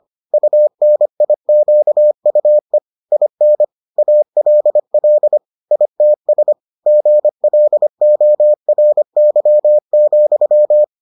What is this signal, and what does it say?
Morse code, heard as a keyed tone, about 25 wpm, 600 Hz. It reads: UNIQUE IN ALL ITS GLORY,